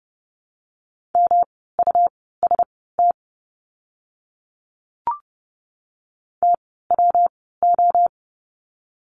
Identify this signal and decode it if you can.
Morse code, heard as a keyed tone, about 30 wpm, spaced as for 17 wpm, 700 Hz. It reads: MUST TWO
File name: course / 30:10wpm / 237 Lesson 38f - Letter q - Words 30wpm